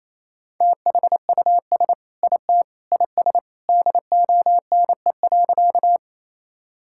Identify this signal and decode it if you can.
Morse code, heard as a keyed tone, about 28 wpm, 700 Hz. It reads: THUS IT IS DONE.
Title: Sentences from Top 1000 Words - Review Part 5 - 28wpm